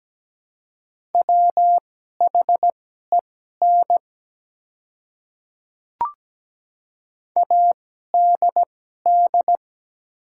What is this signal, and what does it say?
Morse code, heard as a keyed tone, about 17 wpm, spaced as for 12 wpm, 700 Hz. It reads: WHEN ADD
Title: 2x Character Spacing - Top 200 Words - 17wpm